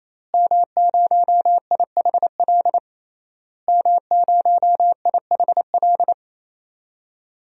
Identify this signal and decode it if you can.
Morse code, heard as a keyed tone, about 28 wpm, 700 Hz. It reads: M0IHL M0IHL
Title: DXCC-100 Call Signs - 5 Characters - Repeated 3x - 28wpm